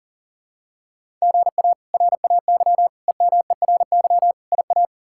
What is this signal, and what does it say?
Morse code, heard as a keyed tone, about 40 wpm, 700 Hz. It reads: GA RAY EMERY IA
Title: K1USN SST QSO Practice - 40wpm